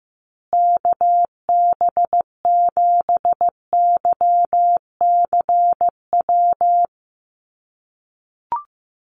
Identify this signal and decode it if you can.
Morse code, heard as a keyed tone, about 15 wpm, 700 Hz. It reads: KB7YCW
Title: USA Call Signs Repeated 2x 15wpm